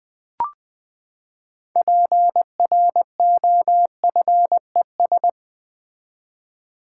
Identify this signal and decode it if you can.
Morse code, heard as a keyed tone, about 20 wpm, 700 Hz. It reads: PROFES
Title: Build A Word - 6 Letters At A Time - 12-Letter Words - 20wpm